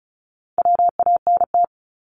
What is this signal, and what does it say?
Morse code, heard as a keyed tone, about 36 wpm, 700 Hz. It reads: WANT
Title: Multi-letter Phonemes Repeated 3x 35wpm